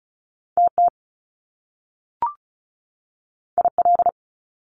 Morse code, heard as a keyed tone, about 36 wpm, 700 Hz. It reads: TT IL
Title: Top Two Letter Combinations in Words 35wpm